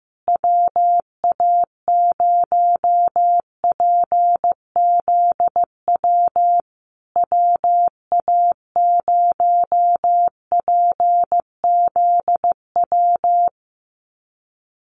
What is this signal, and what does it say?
Morse code, heard as a keyed tone, about 15 wpm, 700 Hz. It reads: WA0PZW WA0PZW
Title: USA Call Signs Repeated 3x 15wpm